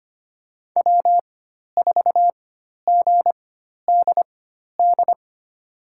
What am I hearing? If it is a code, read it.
Morse code, heard as a keyed tone, about 25 wpm, spaced as for 12 wpm, 700 Hz. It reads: W4GDD
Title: USA Call Signs - 5 Characters - 4x Character Spacing - 25wpm